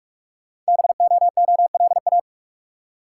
Morse code, heard as a keyed tone, about 44 wpm, 700 Hz. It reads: BOOLA
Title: Code Groups - 5 Letters - 45wpm